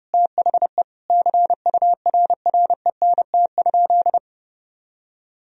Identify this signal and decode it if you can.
Morse code, heard as a keyed tone, about 30 wpm, 700 Hz. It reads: THE CURRENT?
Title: Sentences from Top 1000 Words - Review Part 4 - 30wpm